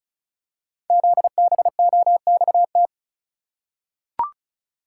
Morse code, heard as a keyed tone, about 36 wpm, 700 Hz. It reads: ZBOXT
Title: Code Groups - 5 Letters - 35wpm